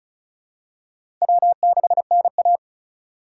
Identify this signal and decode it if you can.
Morse code, heard as a keyed tone, about 36 wpm, 700 Hz. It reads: W6NA
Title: USA Call Signs - 4 Characters - 2x Character Spacing - 35wpm